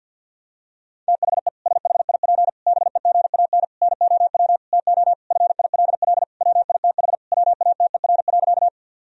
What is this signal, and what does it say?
Morse code, heard as a keyed tone, about 50 wpm, 700 Hz. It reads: THE SHIP BEGAN NOW TO FILL WITH WATER.
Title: Sentences from Top 1000 Words - Review Part 3 - 50wpm